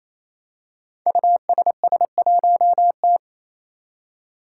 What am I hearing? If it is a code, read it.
Morse code, heard as a keyed tone, about 28 wpm, 700 Hz. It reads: USS1T
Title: Five Challenging Letter-Number Combinations 28wpm